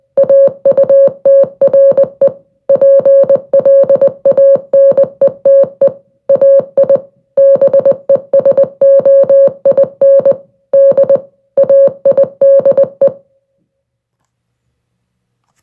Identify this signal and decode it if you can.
Morse code, heard as a keyed tone, about 20 wpm, 550 Hz. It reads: AUTRE PLANETE AI BESOIN D AIDE